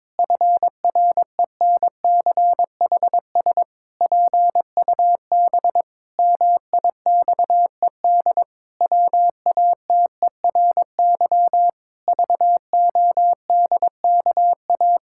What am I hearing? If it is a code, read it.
Morse code, heard as a keyed tone, about 21 wpm, 700 Hz. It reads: FRENCHS PUB MIXED WATERY VODKA
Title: Pangram Challenge v2 - 22wpm